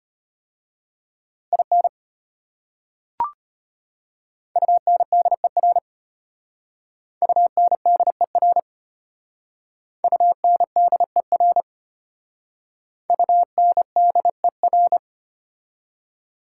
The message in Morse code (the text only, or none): IN UNDER UNDER UNDER UNDER